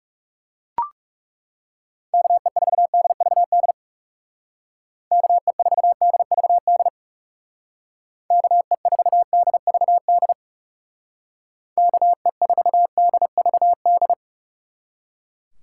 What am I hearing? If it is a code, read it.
Morse code, heard as a keyed tone, about 40 wpm, 700 Hz. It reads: KE4DVD KE4DVD KE4DVD KE4DVD